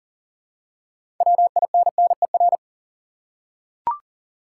Morse code, heard as a keyed tone, about 39 wpm, 700 Hz. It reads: WINNER
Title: Six Letter Words - 40wpm